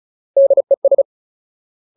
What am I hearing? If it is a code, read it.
Morse code, heard as a keyed tone, about 34 wpm, 550 Hz. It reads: DES
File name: Ultimate Word List / 35WPM Ultimate